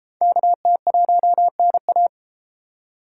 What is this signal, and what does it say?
Morse code, heard as a keyed tone, about 31 wpm, 700 Hz. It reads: KT1NA